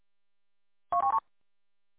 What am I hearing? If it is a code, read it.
There is no code.